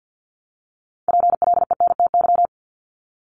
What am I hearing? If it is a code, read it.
Morse code, heard as a keyed tone, about 48 wpm, 700 Hz. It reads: PLENTY
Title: Top 1001-1250 Words 50wpm